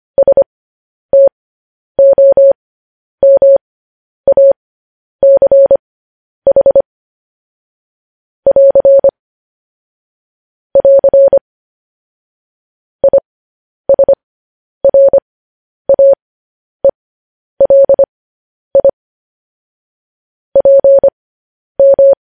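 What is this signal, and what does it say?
Morse code, heard as a keyed tone, about 25 wpm, spaced as for 10 wpm, 550 Hz. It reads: STOMACH <AR> <AR> ISRAELI PM
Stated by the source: Morse code news headlines practice transmission.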